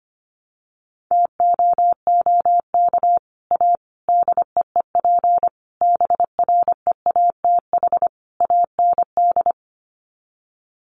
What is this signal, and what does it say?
Morse code, heard as a keyed tone, about 25 wpm, 700 Hz. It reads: TOOK A DEEP BREATH AND
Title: Sets of 5 Words Encore 25wpm